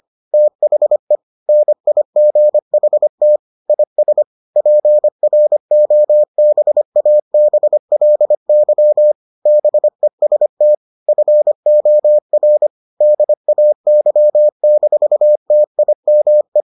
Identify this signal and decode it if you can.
Morse code, heard as a keyed tone, about 25 wpm, 600 Hz. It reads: THE NIGHT IS PROBABLY BEST FOR DAY-TIME